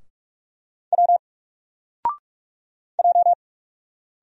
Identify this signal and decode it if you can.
Morse code, heard as a keyed tone, about 46 wpm, 700 Hz. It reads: W J
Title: Single Letter-Number Alt 45wpm